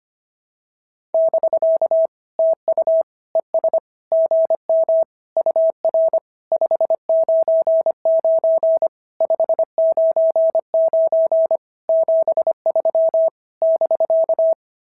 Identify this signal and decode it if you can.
Morse code, heard as a keyed tone, about 23 wpm, 650 Hz. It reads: <BK> TU ES GM UR 599 599 73 <BK>
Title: SOTA QSO Practice 25wpm